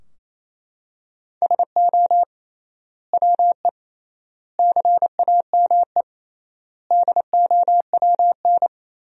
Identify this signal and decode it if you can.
Morse code, heard as a keyed tone, about 28 wpm, 700 Hz. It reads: SO WE CAME DOWN
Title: 3x Word-Spacing - Sentences from Top 400 Words - 28wpm